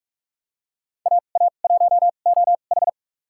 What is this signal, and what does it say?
Morse code, heard as a keyed tone, about 46 wpm, spaced as for 32 wpm, 700 Hz. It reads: AA1OH
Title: USA Call Signs - 2x Character Spacing - 45wpm